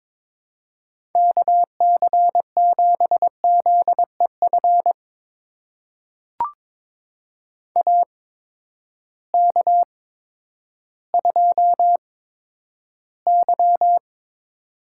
Morse code, heard as a keyed tone, about 22 wpm, spaced as for 11 wpm, 700 Hz. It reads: KC7ZEF AK2Y
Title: USA Call Signs - 8x Character Spacing - 22wpm